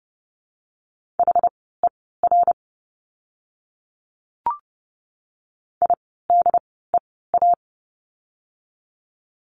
Morse code, heard as a keyed tone, about 30 wpm, spaced as for 17 wpm, 700 Hz. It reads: HER IDEA
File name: Lesson 17f - Letter u - Words 30wpm